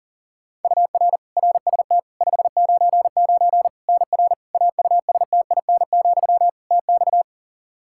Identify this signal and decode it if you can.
Morse code, heard as a keyed tone, about 40 wpm, 700 Hz. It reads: UR RST 599 NR AUSTIN, TX
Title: Practice QSO - Exchange 1 - Beginning - 40wpm